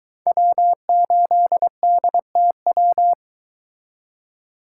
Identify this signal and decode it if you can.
Morse code, heard as a keyed tone, about 22 wpm, 700 Hz. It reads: W8DTW